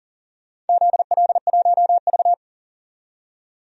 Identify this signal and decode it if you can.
Morse code, heard as a keyed tone, about 40 wpm, 700 Hz. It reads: ZL1V
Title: DXCC-100 Call Signs - 4 Characters - 40wpm